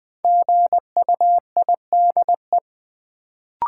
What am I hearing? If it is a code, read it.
Morse code, heard as a keyed tone, about 20 wpm, 700 Hz. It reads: GUIDE
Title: Sentences from Top 600 Words - New Word Drill - 20wpm